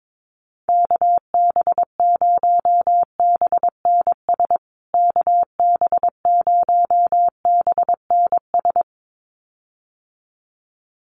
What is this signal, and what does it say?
Morse code, heard as a keyed tone, about 22 wpm, 700 Hz. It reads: KB0BNS KB0BNS
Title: USA Call Signs Repeated 2x 22wpm